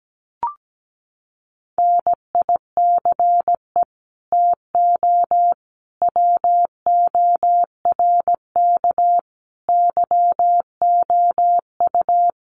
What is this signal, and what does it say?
Morse code, heard as a keyed tone, about 17 wpm, 700 Hz. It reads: NICE TO WORK YOU
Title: Common QSO Phrases 17wpm